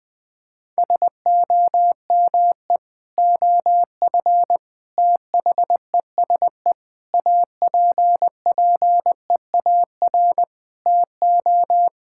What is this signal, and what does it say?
Morse code, heard as a keyed tone, about 21 wpm, 700 Hz. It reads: SOME OF THESE APPEAR TO
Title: Sentences from Top 1000 Words - Review Part 6 - 20wpm